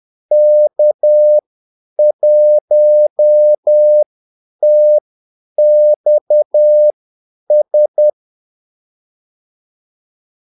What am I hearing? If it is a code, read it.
Morse code, heard as a keyed tone, about 10 wpm, spaced as for 8 wpm, 600 Hz. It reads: K1TXS
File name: Callsigns-10-8wpm